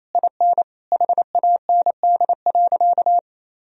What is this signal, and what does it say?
Morse code, heard as a keyed tone, about 28 wpm, 700 Hz. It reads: IN HAND.
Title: Sentences from Top 1000 Words - Review Part 4 - 28wpm